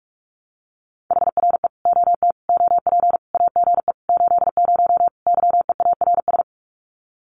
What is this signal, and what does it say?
Morse code, heard as a keyed tone, about 45 wpm, 700 Hz. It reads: HRE OT OP AGE 80 YEARS
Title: Common QSO Phrases 45wpm